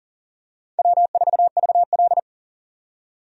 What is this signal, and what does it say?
Morse code, heard as a keyed tone, about 37 wpm, 700 Hz. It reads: W4VL
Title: USA Call Signs - 4 Characters - 40wpm